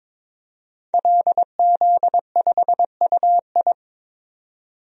Morse code, heard as a keyed tone, about 22 wpm, 700 Hz. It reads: LZ5UI
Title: DXCC-100 Call Signs - 5 Characters - 22wpm